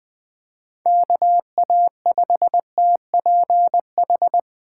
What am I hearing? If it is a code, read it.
Morse code, heard as a keyed tone, about 20 wpm, 700 Hz. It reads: KA5TPH
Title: USA Call Signs - 6 Characters - 2x Character Spacing - 20wpm